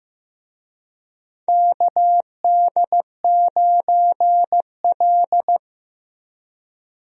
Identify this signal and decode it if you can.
Morse code, heard as a keyed tone, about 15 wpm, 700 Hz. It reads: KD9L
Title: USA Call Signs - 4 Characters - Repeated 3x - 15wpm